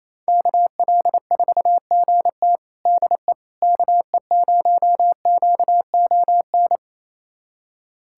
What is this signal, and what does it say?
Morse code, heard as a keyed tone, about 28 wpm, 700 Hz. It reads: KL4GT DE KE0QON
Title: USA Callsign DE Callsign 28wpm